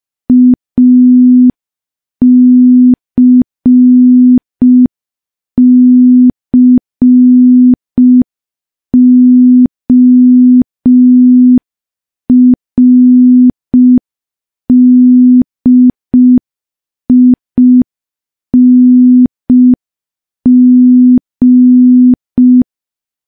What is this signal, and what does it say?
Morse code, heard as a keyed tone, about 5 wpm, 250 Hz. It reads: ACCORDING
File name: morse / morse-training-05WPM